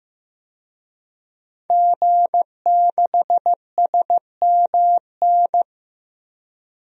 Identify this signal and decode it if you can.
Morse code, heard as a keyed tone, about 15 wpm, 700 Hz. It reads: G6SMN
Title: DXCC-100 Call Signs - 5 Characters - 15wpm